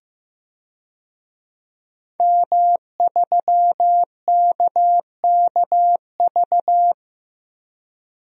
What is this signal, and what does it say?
Morse code, heard as a keyed tone, about 15 wpm, 700 Hz. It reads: M3KKV
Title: DXCC-100 Call Signs - 5 Characters - Repeated 3x - 15wpm